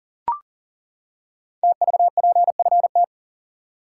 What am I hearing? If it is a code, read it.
Morse code, heard as a keyed tone, about 38 wpm, 700 Hz. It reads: TVPFT
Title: Code Groups - 5 Letters - 40wpm